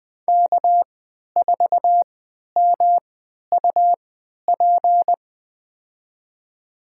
Morse code, heard as a keyed tone, about 20 wpm, spaced as for 11 wpm, 700 Hz. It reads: K4MUP
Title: USA Call Signs - 5 Characters - 3x Character Spacing - 20wpm